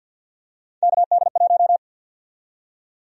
Morse code, heard as a keyed tone, about 52 wpm, 700 Hz. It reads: KD1
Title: USA Call Sign Prefixes - 50wpm